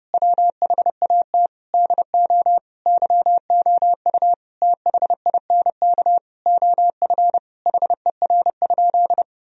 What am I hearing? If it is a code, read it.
Morse code, heard as a keyed tone, about 32 wpm, 700 Hz. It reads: WHAT DO YOU THINK OF HER?
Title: Sentences from Top 1000 Words - Review Part 4 - 30wpm